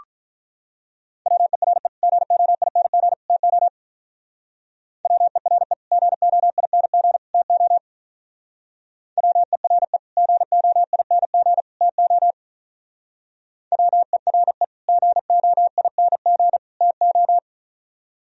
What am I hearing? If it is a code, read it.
Morse code, heard as a keyed tone, about 45 wpm, 700 Hz. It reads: WERE GOING TO WERE GOING TO WERE GOING TO WERE GOING TO